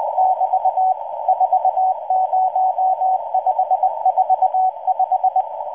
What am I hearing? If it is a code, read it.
Morse code, heard as a keyed tone, about 20 wpm, 750 Hz. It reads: <BT> 40545